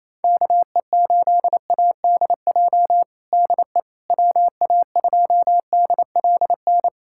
Morse code, heard as a keyed tone, about 28 wpm, 700 Hz. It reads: KE8ADJ DE WA2DLN
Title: USA Callsign DE Callsign - 6 Characters - 28wpm